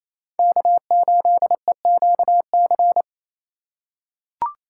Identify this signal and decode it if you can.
Morse code, heard as a keyed tone, about 28 wpm, 700 Hz. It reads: K8EQC
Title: USA Call Signs - 5 Characters - 4x Character Spacing - 28wpm